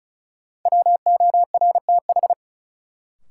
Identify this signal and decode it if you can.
Morse code, heard as a keyed tone, about 34 wpm, 700 Hz. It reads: WORTH
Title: Sentences from Top 800 Words - New Word Drill - 35wpm